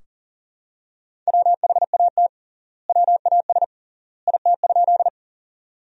Morse code, heard as a keyed tone, about 40 wpm, 700 Hz. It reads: WHAT WAS IT?
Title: 3x Word-Spacing - Sentences from Top 100 Words - 40wpm